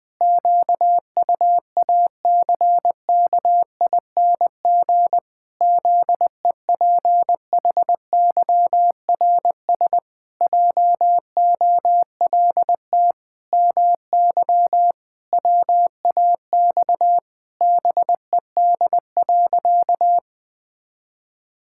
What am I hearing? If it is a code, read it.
Morse code, heard as a keyed tone, about 20 wpm, 700 Hz. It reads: QUACKING ZEPHYRS JOLT MY WAX BED.
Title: Pangram Challenge v2 - 20wpm